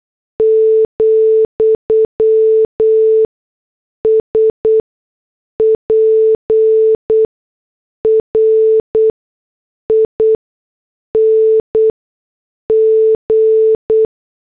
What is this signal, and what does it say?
Morse code, heard as a keyed tone, about 8 wpm, spaced as for 6 wpm, 440 Hz. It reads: , SPRING
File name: morse / headlines